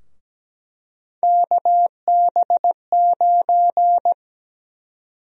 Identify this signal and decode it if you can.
Morse code, heard as a keyed tone, about 17 wpm, 700 Hz. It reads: KB9
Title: USA Call Sign Prefixes - Repeated 2x - 17wpm